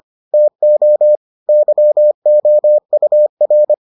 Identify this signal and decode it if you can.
Morse code, heard as a keyed tone, about 25 wpm, 600 Hz. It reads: TO YOUR